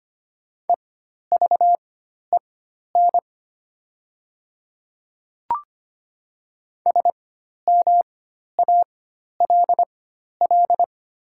Morse code, heard as a keyed tone, about 25 wpm, spaced as for 12 wpm, 700 Hz. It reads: EVEN SMALL
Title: 4x Character Spacing - Top 200 Words - 25wpm